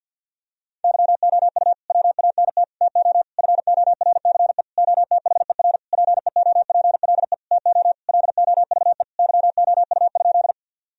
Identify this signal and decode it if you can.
Morse code, heard as a keyed tone, about 50 wpm, 700 Hz. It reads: YOU WANT TO FORCE OTHER PEOPLE TO LOVE YOU?
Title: Sentences from Top 1000 Words - Review Part 3 - 50wpm